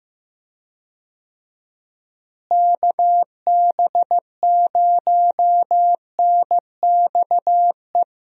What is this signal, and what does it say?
Morse code, heard as a keyed tone, about 15 wpm, 700 Hz. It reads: KB0NXE
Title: USA Call Signs - 6 Characters - Repeated 3x - 15wpm